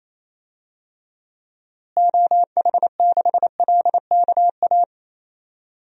Morse code, heard as a keyed tone, about 28 wpm, 700 Hz. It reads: OH6LKA
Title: DXCC-100 Call Signs - 6 Characters - Repeated 3x - 28wpm